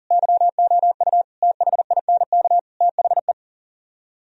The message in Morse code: YOU THINK THE

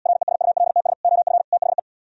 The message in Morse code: RESULTS OF THE

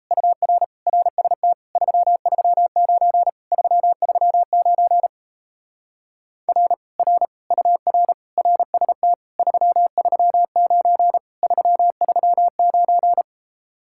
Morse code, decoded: UR RST 339 339 R R UR RST 339 339

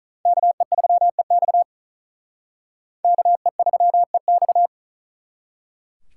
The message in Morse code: KE3EX KE3EX

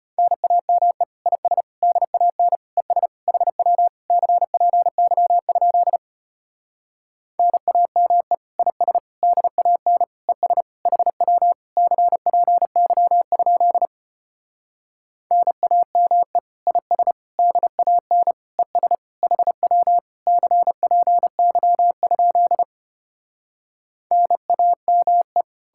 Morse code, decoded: NAME IS DAN ES HW CPY? NAME IS DAN ES HW CPY? NAME IS DAN ES HW CPY? NAME

